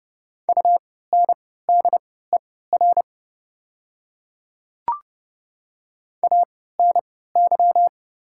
Morse code, heard as a keyed tone, about 30 wpm, spaced as for 17 wpm, 700 Hz. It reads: UNDER ANY
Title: 3x Character Spacing - Top 200 Words - 30wpm